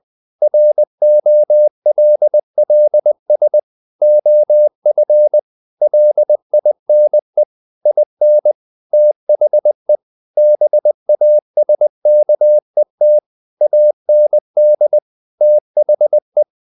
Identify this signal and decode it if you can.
Morse code, heard as a keyed tone, about 20 wpm, 600 Hz. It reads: ROLLS OF LINE IN THE BASKET AND THE
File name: the old man and the sea / the old man and the sea part 1 20wpm